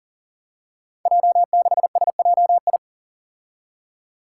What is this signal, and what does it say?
Morse code, heard as a keyed tone, about 40 wpm, 700 Hz. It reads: J6SJI
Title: Five Challenging Letter-Number Combinations 40wpm